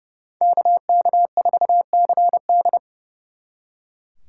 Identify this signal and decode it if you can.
Morse code, heard as a keyed tone, about 37 wpm, 700 Hz. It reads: KK4CD
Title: Speed Racing - USA Call Signs 30wpm